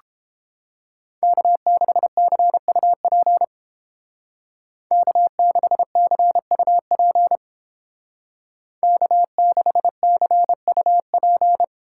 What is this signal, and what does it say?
Morse code, heard as a keyed tone, about 30 wpm, 700 Hz. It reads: K6CUP K6CUP K6CUP